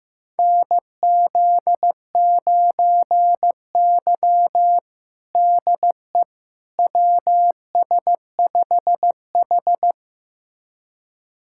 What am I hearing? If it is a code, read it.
Morse code, heard as a keyed tone, about 15 wpm, 700 Hz. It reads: NZ9Y DE WS5H
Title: USA Callsign DE Callsign - 4 Characters - 15wpm